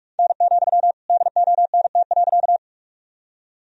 Morse code, heard as a keyed tone, about 43 wpm, 700 Hz. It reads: N, DONT.
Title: Sentences from Top 300 Words and Abbreviations 45wpm